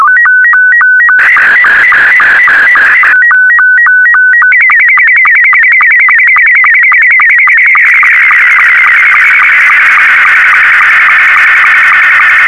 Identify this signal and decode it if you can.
SSTV (slow-scan television), Scottie 2.